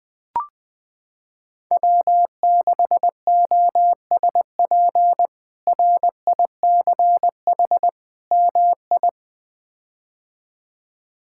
Morse code, heard as a keyed tone, about 20 wpm, 700 Hz. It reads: W6OSP RICH MI
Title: K1USN SST QSO Practice - 20wpm